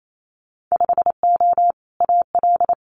Morse code, heard as a keyed tone, about 28 wpm, 700 Hz. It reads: 5O AL
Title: Winter Field Day Exchanges 28wpm